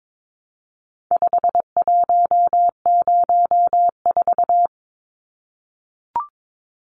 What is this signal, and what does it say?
Morse code, heard as a keyed tone, about 22 wpm, 700 Hz. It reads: 5104
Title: Numbers - 4-Digit 22wpm